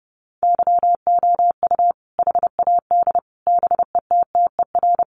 Morse code, heard as a keyed tone, about 30 wpm, 700 Hz. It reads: YOU HAD BETTER
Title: Sentences from Top 700 Words 30wpm